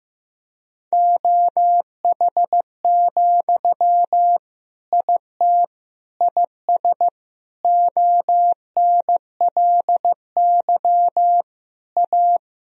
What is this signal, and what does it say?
Morse code, heard as a keyed tone, about 15 wpm, 700 Hz. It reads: OH, IT IS ONLY A
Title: Sentences from Top 1000 Words - Review Part 3 - 15wpm